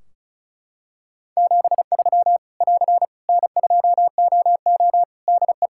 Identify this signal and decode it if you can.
Morse code, heard as a keyed tone, about 36 wpm, 700 Hz. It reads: 73 <AR> N2OO DE
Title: Practice QSO - Exchange 4 - Ending - 35wpm